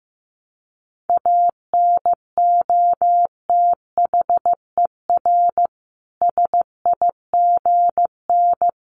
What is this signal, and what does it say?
Morse code, heard as a keyed tone, about 15 wpm, 700 Hz. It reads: ANOTHER SIGN
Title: Sentences from Top 1000 Words 15wpm